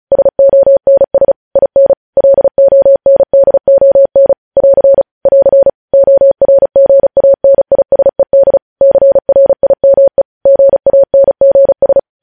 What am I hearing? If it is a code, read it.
Morse code, heard as a keyed tone, about 36 wpm, 550 Hz. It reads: SONS IN LONDON <AR> <AR> ORGANISED CRIME GANGS
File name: podcast / News headlines-1x-35x35